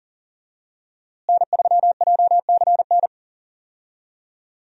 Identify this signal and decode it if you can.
Morse code, heard as a keyed tone, about 37 wpm, 700 Hz. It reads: N3JCN
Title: USA Call Signs - 5 Characters - Repeated 2x - 40wpm